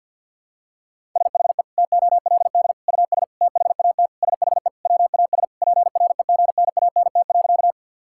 Morse code, heard as a keyed tone, about 52 wpm, 700 Hz. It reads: SHE TOLD US THAT SHE WAS PREGNANT.
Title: Sentences from Top 1000 Words - Review Part 6 - 50wpm